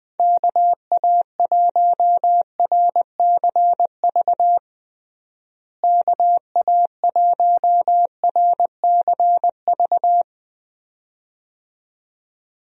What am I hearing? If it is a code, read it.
Morse code, heard as a keyed tone, about 20 wpm, 700 Hz. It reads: KA1RCV KA1RCV
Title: USA Call Signs - 6 Characters - Repeated 2x - 20wpm